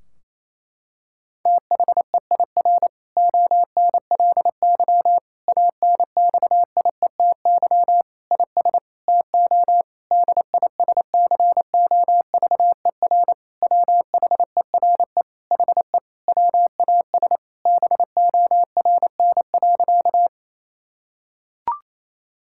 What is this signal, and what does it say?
Morse code, heard as a keyed tone, about 28 wpm, 700 Hz. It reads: THEIR ONLY ANXIETY IS TO DISCOVER WHERE HE WAS BORN.
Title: Sentences from Top 1600 Words 28wpm